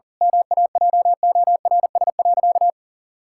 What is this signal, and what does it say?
Morse code, heard as a keyed tone, about 40 wpm, 700 Hz. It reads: MAJORS.
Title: Pangram Challenge v2 - 40wpm